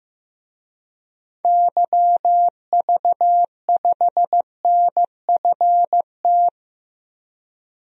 Morse code, heard as a keyed tone, about 16 wpm, 700 Hz. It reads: YV5NFT
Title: DXCC-100 Call Signs - 6 Characters - 15wpm